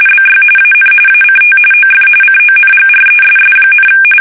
RTTY, about 45.45 baud. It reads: BY AN ACT OF CONGRESS,
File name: fsk850 / battleships